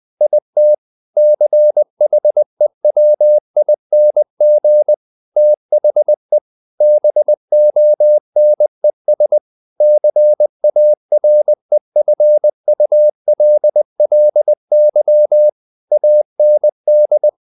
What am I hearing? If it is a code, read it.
Morse code, heard as a keyed tone, about 20 wpm, 600 Hz. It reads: IT CHEWING THE BONES CAREFULLY AND
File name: the old man and the sea part 2 20wpm